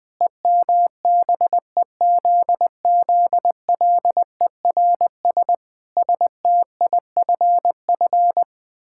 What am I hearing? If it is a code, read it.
Morse code, heard as a keyed tone, about 20 wpm, 700 Hz. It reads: EMBEZZLERS STIFF
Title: Pangram Challenge v2 - 20wpm